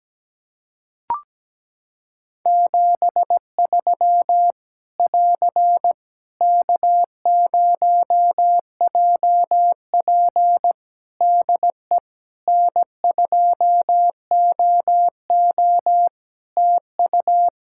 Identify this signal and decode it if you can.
Morse code, heard as a keyed tone, about 17 wpm, 700 Hz. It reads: 73 <AR> K0JP DE N2OO TU
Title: Practice QSO - Exchange 4 - Ending - 17wpm